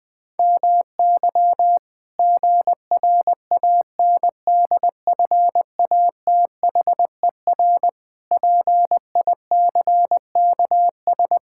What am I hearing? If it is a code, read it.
Morse code, heard as a keyed tone, about 20 wpm, 700 Hz. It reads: MY GRANDFATHER PICKS
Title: Pangram Challenge v2 - 20wpm